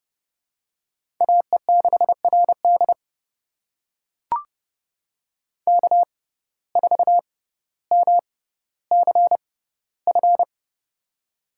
Morse code, heard as a keyed tone, about 30 wpm, spaced as for 15 wpm, 700 Hz. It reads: AE6RD K4MCF
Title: USA Call Signs - 5 Characters - 6x Character Spacing - 30wpm